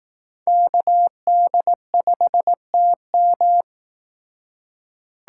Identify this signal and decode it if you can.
Morse code, heard as a keyed tone, about 18 wpm, 700 Hz. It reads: KD5TM
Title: Speed Racing - USA Call Signs 15wpm